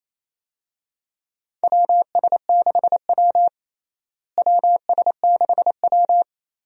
Morse code, heard as a keyed tone, about 28 wpm, 700 Hz. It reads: WS6W WS6W
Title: USA Call Signs - 4 Characters - Repeated 3x - 28wpm